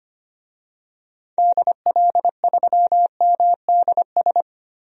Morse code, heard as a keyed tone, about 25 wpm, 700 Hz. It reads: DL3MDS
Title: DXCC-100 Call Signs - 6 Characters - 25wpm